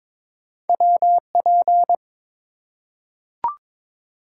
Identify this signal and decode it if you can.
Morse code, heard as a keyed tone, about 22 wpm, 700 Hz. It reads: WP
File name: practice / 22wpm / Two Letter-Number Combinations - 22wpm